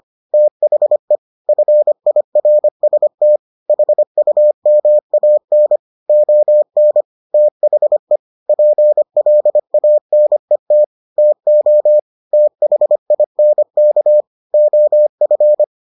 Morse code, heard as a keyed tone, about 25 wpm, 600 Hz. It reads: THE FIRST HUMAN ON THE PLANET TO THINK OF